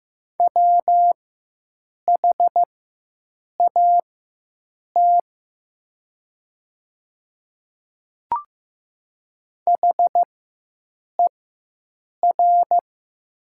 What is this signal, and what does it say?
Morse code, heard as a keyed tone, about 15 wpm, spaced as for 7 wpm, 700 Hz. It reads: WHAT HER
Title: 4x Character Spacing - Top 100 Words - 15wpm